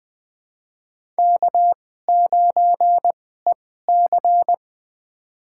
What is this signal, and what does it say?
Morse code, heard as a keyed tone, about 20 wpm, spaced as for 14 wpm, 700 Hz. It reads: K9EC
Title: USA Call Signs - 2x Character Spacing - 20wpm